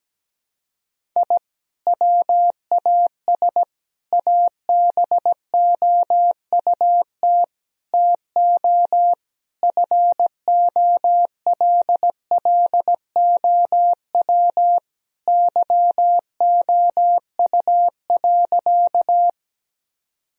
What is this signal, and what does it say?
Morse code, heard as a keyed tone, about 17 wpm, 700 Hz. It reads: I WAS ABOUT TO FOLLOW YOU.
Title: Sentences from Top 1000 Words - Review Part 1 - 17wpm